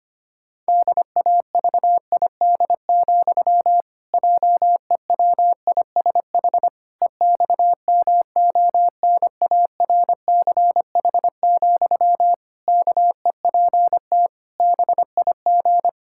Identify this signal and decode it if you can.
Morse code, heard as a keyed tone, about 25 wpm, 700 Hz. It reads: DAVID, JEWISH EXMONARCH, KEPT BIG
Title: Pangram Challenge v2 - 25wpm